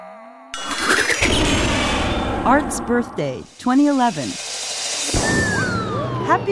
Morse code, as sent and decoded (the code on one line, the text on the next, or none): none